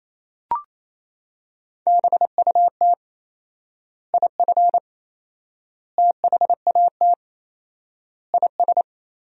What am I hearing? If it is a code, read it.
Morse code, heard as a keyed tone, about 28 wpm, 700 Hz. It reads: BUT IF THAT IS
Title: 4x Word-Spacing - Sentences from Top 200 Words - 28wpm